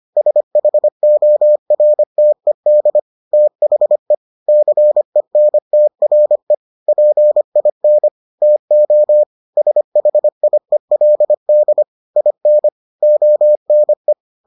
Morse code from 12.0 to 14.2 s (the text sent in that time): IN ONE